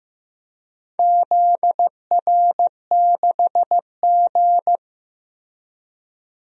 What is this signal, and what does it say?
Morse code, heard as a keyed tone, about 15 wpm, 700 Hz. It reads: ZR6G
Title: DXCC-100 Call Signs - 4 Characters - 15wpm